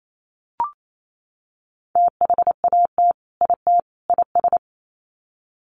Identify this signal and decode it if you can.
Morse code, heard as a keyed tone, about 28 wpm, 700 Hz. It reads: THAT IT IS
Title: Sets of 3 Words 28wpm